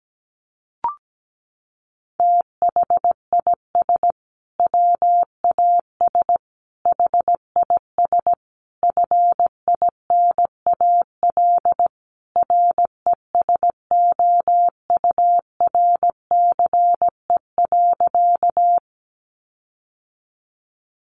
Morse code, heard as a keyed tone, about 17 wpm, 700 Hz. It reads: THIS WAS HIS FINAL RESOURCE.
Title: Sentences from Top 1300 Words 17wpm